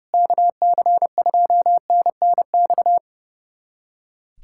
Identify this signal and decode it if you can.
Morse code, heard as a keyed tone, about 30 wpm, 700 Hz. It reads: KC2NNX
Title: USA Callsign DE Callsign 30wpm